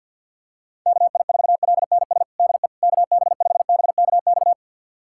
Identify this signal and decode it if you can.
Morse code, heard as a keyed tone, about 52 wpm, 700 Hz. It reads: KI4LNS DE KB5BKX
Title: USA Callsign DE Callsign 50wpm